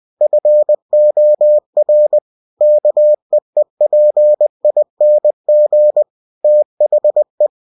Morse code, heard as a keyed tone, about 20 wpm, 600 Hz. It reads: FOR KEEPING THE
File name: cw-books / winnie-the-pooh / winnie-20wpm-page-0114